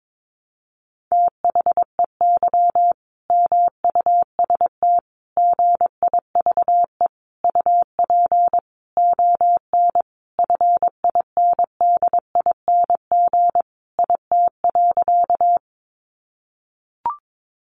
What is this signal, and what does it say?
Morse code, heard as a keyed tone, about 22 wpm, 700 Hz. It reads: THEY MUST GIVE UP ON FINDING IT.
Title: Sentences from Top 1100 Words 22wpm